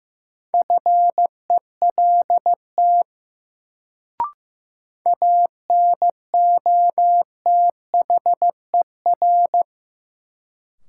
Morse code, heard as a keyed tone, about 15 wpm, 700 Hz. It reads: FELT ANOTHER
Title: Sets of 5 Words Encore - New Word Drill - 15wpm